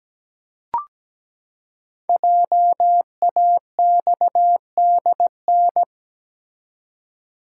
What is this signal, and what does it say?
Morse code, heard as a keyed tone, about 17 wpm, 700 Hz. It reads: JAXDN
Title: Code Groups - 5 Letters - 17wpm